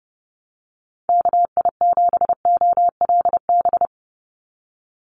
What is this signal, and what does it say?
Morse code, heard as a keyed tone, about 30 wpm, 700 Hz. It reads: KI7OLB
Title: USA Call Signs 30wpm